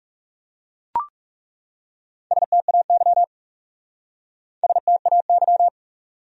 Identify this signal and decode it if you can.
Morse code, heard as a keyed tone, about 39 wpm, 700 Hz. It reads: STAY STAY